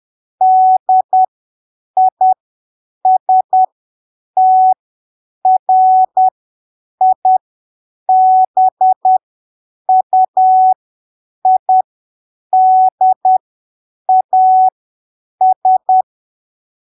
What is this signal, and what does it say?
Morse code, heard as a keyed tone, about 10 wpm, spaced as for 7 wpm, 750 Hz. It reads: DISTRIBUIDAS
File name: arquivos / texto / texto 06 10ppm